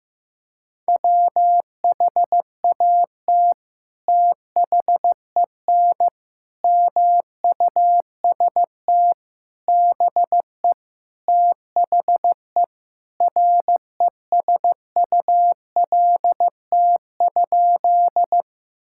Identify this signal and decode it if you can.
Morse code, heard as a keyed tone, about 15 wpm, 700 Hz. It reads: WHAT THEN MUST BE THE RESULT?
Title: Sentences from Top 1000 Words - Review Part 5 - 15wpm